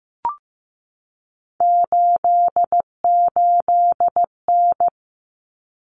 Morse code, heard as a keyed tone, about 15 wpm, 700 Hz. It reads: 88N
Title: Common Cut Numbers - 3-Digit 15wpm